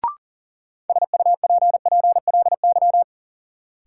Morse code, heard as a keyed tone, about 40 wpm, 700 Hz. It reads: SUPPLY
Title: Sentences from Top 1200 Words - New Word Drill Alt - 40wpm